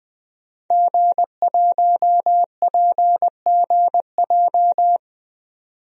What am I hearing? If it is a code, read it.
Morse code, heard as a keyed tone, about 20 wpm, 700 Hz. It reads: G1PGJ
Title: DXCC-100 Call Signs - 5 Characters - 20wpm